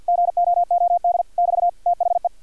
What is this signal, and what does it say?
Morse code, heard as a keyed tone, about 50 wpm, 700 Hz, with radio noise. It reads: GOOD <BT> THE
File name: archived / CRSnet